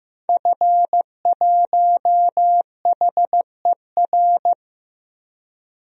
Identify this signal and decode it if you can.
Morse code, heard as a keyed tone, about 15 wpm, 700 Hz. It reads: F1HER
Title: DXCC-100 Call Signs - 5 Characters - 15wpm